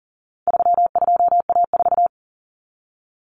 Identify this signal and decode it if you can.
Morse code, heard as a keyed tone, about 40 wpm, 700 Hz. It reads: 32A4
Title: Common Cut Numbers - 4-Digit 40wpm